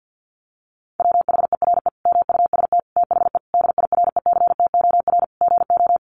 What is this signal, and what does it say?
Morse code, heard as a keyed tone, about 50 wpm, 700 Hz. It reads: WHERE MUST THE DIRECTOR GO
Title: Sentences from Top 1300 Words 50wpm